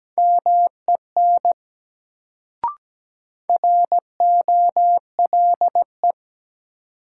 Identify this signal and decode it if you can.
Morse code, heard as a keyed tone, about 17 wpm, 700 Hz. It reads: MEN ROLE
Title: Sets of 3 Words - New Word Drill - 17wpm